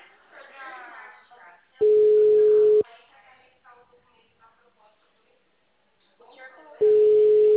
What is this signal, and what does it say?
Ringback tone (Europe, 425 Hz), with speech.